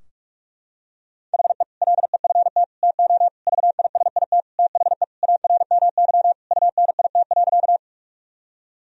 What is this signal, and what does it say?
Morse code, heard as a keyed tone, about 46 wpm, 700 Hz. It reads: HE LEFT TO VISIT THE ARMY UNIT.
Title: Sentences from Top 1000 Words - Review Part 2 - 45wpm